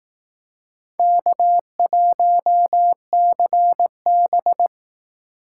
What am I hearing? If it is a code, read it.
Morse code, heard as a keyed tone, about 19 wpm, 700 Hz. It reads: K1CB